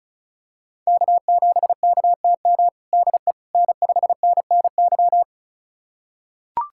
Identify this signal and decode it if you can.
Morse code, heard as a keyed tone, about 36 wpm, 700 Hz. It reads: K7KTM DE N5NNY
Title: USA Callsign DE Callsign 35wpm